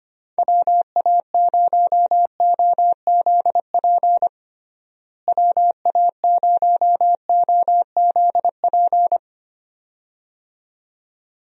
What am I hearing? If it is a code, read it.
Morse code, heard as a keyed tone, about 25 wpm, 700 Hz. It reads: WA0OZP WA0OZP
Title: USA Call Signs - 6 Characters - Repeated 2x - 25wpm